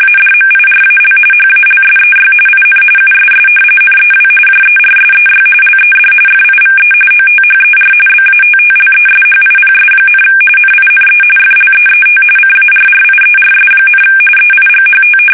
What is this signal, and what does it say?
RTTY, about 45.45 baud. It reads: ADMIRAL HALSEY IN USS NEW JERSEY LED FIVE OTHER BATTLESHIPS, NINE FLEET CARRIERS, EIGHT
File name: fsk850 / battleshipnj